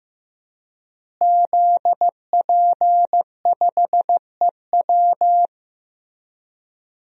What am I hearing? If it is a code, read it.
Morse code, heard as a keyed tone, about 15 wpm, 700 Hz. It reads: ZP5EW
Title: DXCC-100 Call Signs - 5 Characters - Repeated 2x - 15wpm